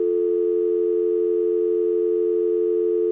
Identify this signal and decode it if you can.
Telephone call progress tone (North America), heard as a telephone network's tone, about 350, 440 Hz. Dial tone.